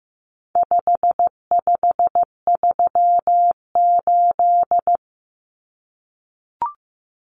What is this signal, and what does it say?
Morse code, heard as a keyed tone, about 15 wpm, 700 Hz. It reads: 5538
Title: Common Cut Numbers - 4-Digit 15wpm